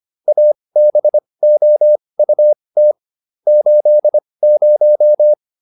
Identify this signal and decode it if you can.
Morse code, heard as a keyed tone, about 25 wpm, spaced as for 20 wpm, 600 Hz. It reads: ABOUT 80